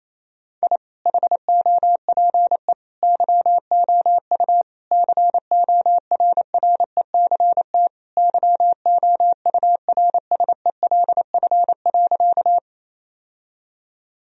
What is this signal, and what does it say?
Morse code, heard as a keyed tone, about 28 wpm, 700 Hz. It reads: I HOPE YOU CORRECT YOURSELF.
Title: Sentences from Top 1000 Words - Review Part 2 - 28wpm